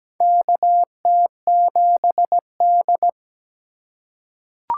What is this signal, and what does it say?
Morse code, heard as a keyed tone, about 17 wpm, 700 Hz. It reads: KT7D
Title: USA Call Signs - 4x Character Spacing - 17wpm